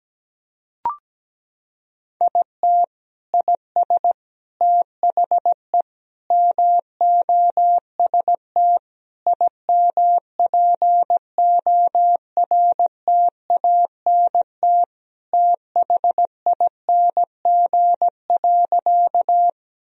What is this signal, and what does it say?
Morse code, heard as a keyed tone, about 17 wpm, 700 Hz. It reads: IT IS THE MOST IMPORTANT THING.
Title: Sentences from Top 1000 Words - Review Part 1 - 17wpm